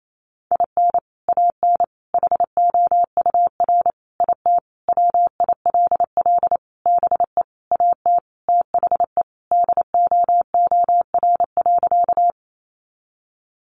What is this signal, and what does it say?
Morse code, heard as a keyed tone, about 28 wpm, 700 Hz. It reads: IN AN HOUR IT WILL BE AT THE DOOR.
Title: Sentences from Top 400 Words 28wpm